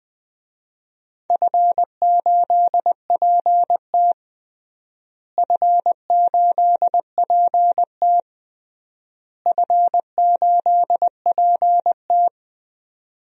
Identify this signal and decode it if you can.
Morse code, heard as a keyed tone, about 20 wpm, 700 Hz. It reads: F8PT F8PT F8PT
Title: DXCC-100 Call Signs - 4 Characters - Repeated 3x - 20wpm